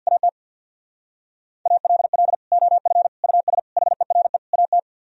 Morse code, heard as a keyed tone, about 65 wpm, 700 Hz. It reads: AT ALL OF US HERE AT